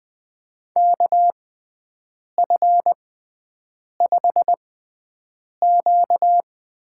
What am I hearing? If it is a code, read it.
Morse code, heard as a keyed tone, about 20 wpm, spaced as for 7 wpm, 700 Hz. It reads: KF5Q
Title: USA Call Signs - 6x Character Spacing - 20wpm